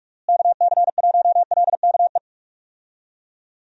Morse code, heard as a keyed tone, about 41 wpm, 700 Hz. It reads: KC1LKE
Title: USA Call Signs - 6 Characters - Repeated 3x - 45wpm